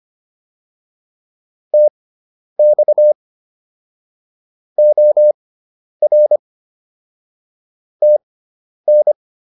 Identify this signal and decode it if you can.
Morse code, heard as a keyed tone, about 25 wpm, spaced as for 10 wpm, 600 Hz. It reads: TX OR TN